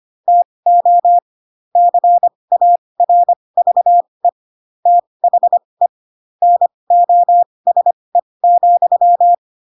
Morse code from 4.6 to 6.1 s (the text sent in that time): THE